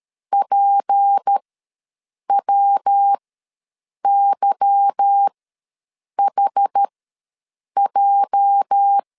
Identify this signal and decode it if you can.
Morse code, heard as a keyed tone, about 13 wpm, spaced as for 7 wpm, 800 Hz. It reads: PWYHJ
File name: LEZIONI / 8 YJ45 / 8 YJ ws 12wpm 3342